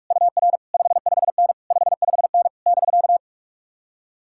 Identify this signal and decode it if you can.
Morse code, heard as a keyed tone, about 46 wpm, 700 Hz. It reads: UR 55N 55N <BK>
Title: Speed-Racing Common QSO Phrases - 45wpm